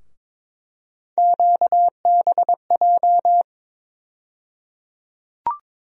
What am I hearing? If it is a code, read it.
Morse code, heard as a keyed tone, about 22 wpm, 700 Hz. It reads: QBJ
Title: USA Call Sign Postfixes - 3 Letters 22wpm